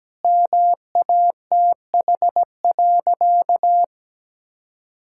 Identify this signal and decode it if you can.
Morse code, heard as a keyed tone, about 17 wpm, 700 Hz. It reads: MATH.
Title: Sentences from Top 1000 Words - Review Part 6 - 17wpm